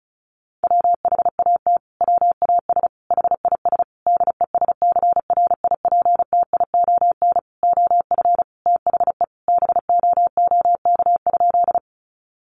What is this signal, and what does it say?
Morse code, heard as a keyed tone, about 34 wpm, 700 Hz. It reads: WHAT WAS HIS DESCRIPTION OF THE BOOK?
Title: Sentences from Top 1300 Words 35wpm